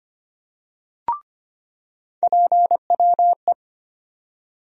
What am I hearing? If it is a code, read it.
Morse code, heard as a keyed tone, about 25 wpm, 700 Hz. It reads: PWE
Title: USA Call Sign Postfixes - 3 Letters 25wpm